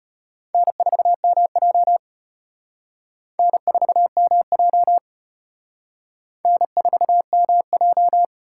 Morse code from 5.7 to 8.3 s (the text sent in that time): N4MJ